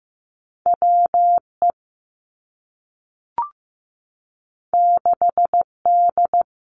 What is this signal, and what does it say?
Morse code, heard as a keyed tone, about 15 wpm, 700 Hz. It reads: WE 6D
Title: Two Challenging Letter-Number Combinations 15wpm